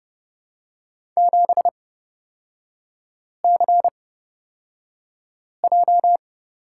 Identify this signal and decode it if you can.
Morse code, heard as a keyed tone, about 30 wpm, spaced as for 5 wpm, 700 Hz. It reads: 7CJ
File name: Lesson 37e - Letter x - Three Characters 30wpm